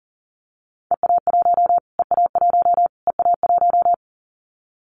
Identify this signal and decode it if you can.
Morse code, heard as a keyed tone, about 40 wpm, 700 Hz. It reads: EA1 EA1 EA1
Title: Callsign Prefixes for Top 25 DXCC - Repeated 3x - 40wpm